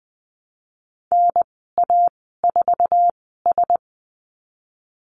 Morse code, heard as a keyed tone, about 19 wpm, spaced as for 14 wpm, 700 Hz. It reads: NA4S
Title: 055 Lesson 10g - Letter r - Call Signs 20wpm